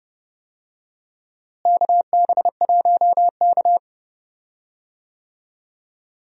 Morse code, heard as a keyed tone, about 31 wpm, 700 Hz. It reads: KB1K
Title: USA Call Signs - 4 Characters - Repeated 2x - 30wpm